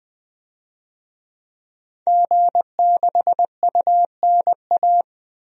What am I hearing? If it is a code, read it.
Morse code, heard as a keyed tone, about 20 wpm, 700 Hz. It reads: G6UNA
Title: DXCC-100 Call Signs - 5 Characters - 20wpm